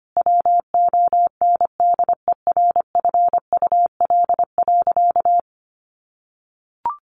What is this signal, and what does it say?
Morse code, heard as a keyed tone, about 25 wpm, 700 Hz. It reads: WONDERFUL.
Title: Sentences from Top 1100 Words 25wpm